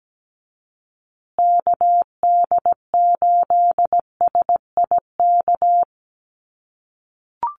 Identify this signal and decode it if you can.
Morse code, heard as a keyed tone, about 17 wpm, 700 Hz. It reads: KD8SIK
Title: USA Call Signs Repeated 3x 17wpm